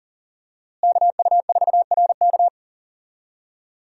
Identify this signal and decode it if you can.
Morse code, heard as a keyed tone, about 40 wpm, 700 Hz. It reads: KU4RK
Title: USA Call Signs - 5 Characters - Repeated 2x - 40wpm